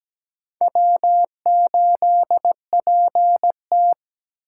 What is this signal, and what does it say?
Morse code, heard as a keyed tone, about 17 wpm, 700 Hz. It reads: W8PT
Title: USA Call Signs - 2x Character Spacing - 17wpm